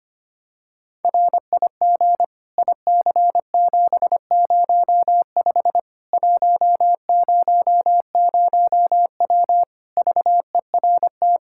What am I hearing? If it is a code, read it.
Morse code, heard as a keyed tone, about 25 wpm, 700 Hz. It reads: RIG IC705 100W VERT
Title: K1USN SST QSO Practice 25wpm